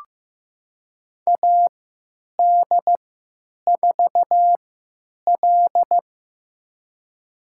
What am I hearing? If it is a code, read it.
Morse code, heard as a keyed tone, about 15 wpm, spaced as for 9 wpm, 700 Hz. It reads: AD4L
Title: USA Call Signs - 3x Character Spacing - 15wpm